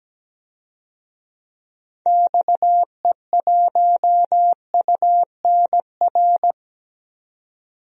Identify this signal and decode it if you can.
Morse code, heard as a keyed tone, about 17 wpm, 700 Hz. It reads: XE1UNR
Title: DXCC-100 Call Signs - 6 Characters - 17wpm